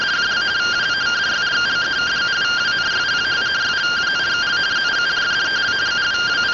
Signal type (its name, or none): two-tone teleprinter signal